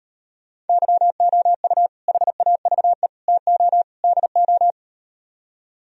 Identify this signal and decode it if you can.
Morse code, heard as a keyed tone, about 39 wpm, 700 Hz. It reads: YOU HAVE TO DO